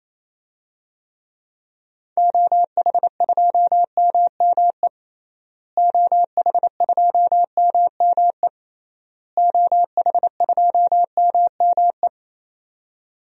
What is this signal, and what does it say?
Morse code, heard as a keyed tone, about 28 wpm, 700 Hz. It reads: OH2MME OH2MME OH2MME
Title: DXCC-100 Call Signs - 6 Characters - Repeated 3x - 28wpm